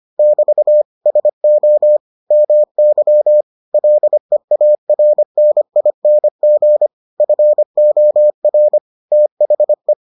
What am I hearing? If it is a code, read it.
Morse code, heard as a keyed tone, about 25 wpm, 600 Hz. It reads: <BT> SO MY LEARNING FOR THE